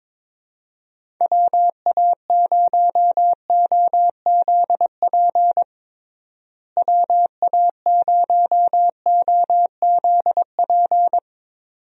Morse code, heard as a keyed tone, about 22 wpm, 700 Hz. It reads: WA0OZP WA0OZP
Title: USA Call Signs - 6 Characters - Repeated 2x - 22wpm